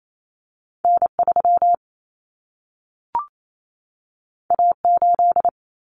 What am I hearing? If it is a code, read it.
Morse code, heard as a keyed tone, about 28 wpm, 700 Hz. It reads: N3 A8
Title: Common Cut Numbers - 2-Digit 28wpm